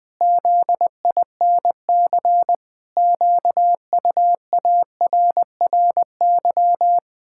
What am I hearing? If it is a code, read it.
Morse code, heard as a keyed tone, about 20 wpm, 700 Hz. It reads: ZINC QUARRY
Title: Pangram Challenge v2 - 20wpm